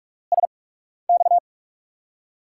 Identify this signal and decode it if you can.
Morse code, heard as a keyed tone, about 42 wpm, spaced as for 12 wpm, 700 Hz. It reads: SX
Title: USA Call Signs - 8x Character Spacing - 45wpm